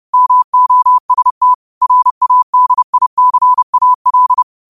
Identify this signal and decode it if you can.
Morse code, heard as a keyed tone, about 30 wpm, 1000 Hz. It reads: MOST RADICAL
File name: qotdfiles / 30wpm